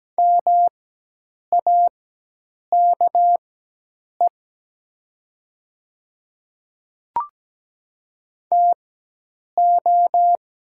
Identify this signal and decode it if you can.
Morse code, heard as a keyed tone, about 17 wpm, spaced as for 8 wpm, 690 Hz. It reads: MAKE TO
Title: 4x Character Spacing - Top 100 Words - 17wpm